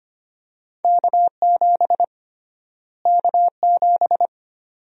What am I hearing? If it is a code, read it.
Morse code, heard as a keyed tone, about 25 wpm, 700 Hz. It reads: K7 K7
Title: USA Call Sign Prefixes - Repeated 2x - 25wpm